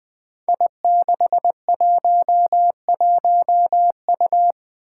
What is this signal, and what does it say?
Morse code, heard as a keyed tone, about 20 wpm, 700 Hz. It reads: I611U
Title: Five Challenging Letter-Number Combinations 20wpm